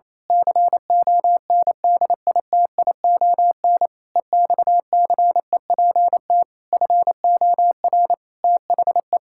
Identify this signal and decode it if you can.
Morse code, heard as a keyed tone, about 27 wpm, 700 Hz. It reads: CONDITION EXCEPT FOR THE
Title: Pangram Challenge v2 - 28wpm